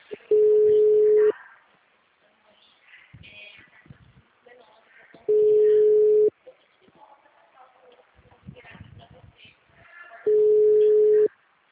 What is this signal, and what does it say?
Ringback tone (Europe, 425 Hz), with speech.
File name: out- -2050- - - .25659